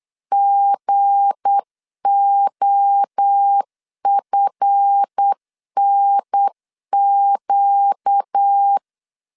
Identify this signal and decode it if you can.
Morse code, heard as a keyed tone, about 8 wpm, 800 Hz. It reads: GOFNQ
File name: LEZIONI / 7 GF9Z / 7 GF 8wpm 3108